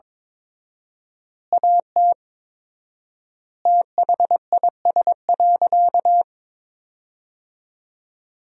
Morse code, heard as a keyed tone, about 22 wpm, 700 Hz. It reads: AT THIS.
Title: 4x Word-Spacing - Sentences from Top 300 Words - 22wpm